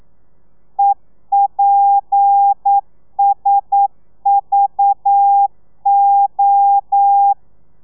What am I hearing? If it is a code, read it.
Morse code, heard as a keyed tone, about 9 wpm, 800 Hz. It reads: EPSVO
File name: telegrafia / i2tfj / Lez7 40